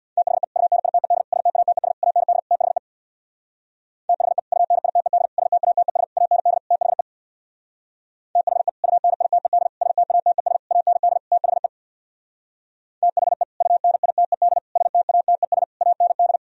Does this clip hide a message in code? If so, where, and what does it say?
Yes: Morse code from 0.2 to 16.4 s, heard as a keyed tone, about 64 wpm, 700 Hz. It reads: THE UNITED STATES AND THE THE UNITED STATES AND THE THE UNITED STATES AND THE THE UNITED STATES AND